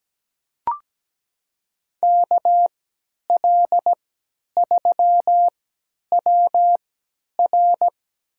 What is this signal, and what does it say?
Morse code, heard as a keyed tone, about 17 wpm, spaced as for 10 wpm, 700 Hz. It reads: KL3WR
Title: USA Call Signs - 5 Characters - 3x Character Spacing - 17wpm